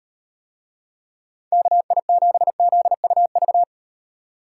Morse code, heard as a keyed tone, about 37 wpm, 700 Hz. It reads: KI7ZUV